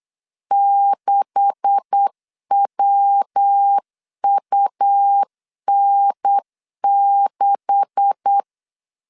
Morse code, heard as a keyed tone, about 8 wpm, 800 Hz. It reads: 6WUN6
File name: LEZIONI / 9 286 8wpm 3425